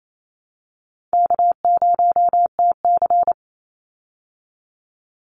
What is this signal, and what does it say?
Morse code, heard as a keyed tone, about 28 wpm, 700 Hz. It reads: K0TC
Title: Easy Call Signs Repeated 3x - 28wpm